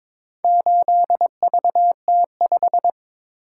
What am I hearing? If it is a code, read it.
Morse code, heard as a keyed tone, about 22 wpm, 700 Hz. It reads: 8VT5
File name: practice / 22wpm / Four Letter-Number Combinations - 22wpm